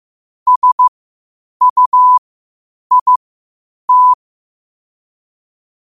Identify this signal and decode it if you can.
Morse code, heard as a keyed tone, about 15 wpm, spaced as for 8 wpm, 1000 Hz. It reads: SUIT
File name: qotdfiles / 8wpm